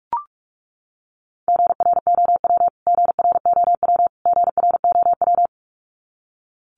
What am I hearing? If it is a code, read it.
Morse code, heard as a keyed tone, about 44 wpm, 700 Hz. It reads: GROW GROW GROW
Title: Top 300 Words Repeated 45wpm